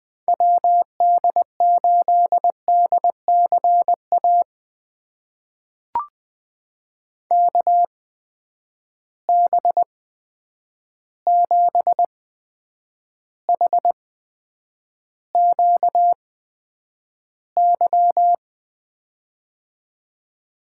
Morse code, heard as a keyed tone, about 20 wpm, spaced as for 9 wpm, 700 Hz. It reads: WD8DCA KB7HQY
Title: USA Call Signs - 6 Characters - 8x Character Spacing - 20wpm